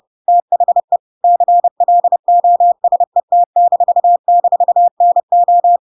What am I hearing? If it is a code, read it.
Morse code, heard as a keyed tone, about 30 wpm, 700 Hz. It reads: THE CLOSET--NO
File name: Pinocchio / Pinocchio ch1 30 30 ik1hge